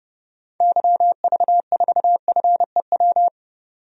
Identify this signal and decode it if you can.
Morse code, heard as a keyed tone, about 30 wpm, 700 Hz. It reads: YV4FEW
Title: DXCC-100 Call Signs - 6 Characters - 30wpm